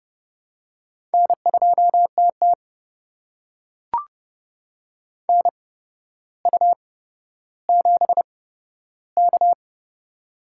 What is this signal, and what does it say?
Morse code, heard as a keyed tone, about 30 wpm, spaced as for 13 wpm, 700 Hz. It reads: N2TT NU7K
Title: USA Call Signs - 8x Character Spacing - 30wpm